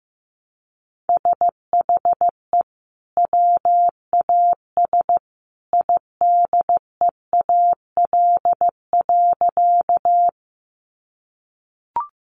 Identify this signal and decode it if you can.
Morse code, heard as a keyed tone, about 15 wpm, 700 Hz. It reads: SHE WAS IDEAL.
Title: Sentences from Top 1400 Words 15wpm